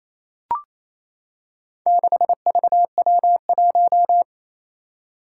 Morse code, heard as a keyed tone, about 27 wpm, 700 Hz. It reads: 6VW1
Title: Four Challenging Letter-Number Combinations 28wpm